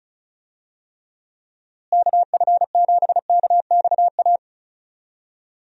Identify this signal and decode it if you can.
Morse code, heard as a keyed tone, about 36 wpm, 700 Hz. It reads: KF7KXA
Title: USA Call Signs - 6 Characters - Repeated 3x - 35wpm